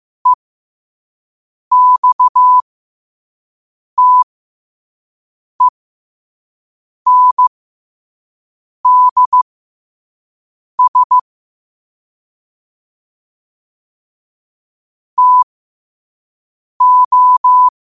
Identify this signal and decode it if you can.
Morse code, heard as a keyed tone, about 15 wpm, spaced as for 5 wpm, 1000 Hz. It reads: EXTENDS TO
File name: qotdfiles / 5wpm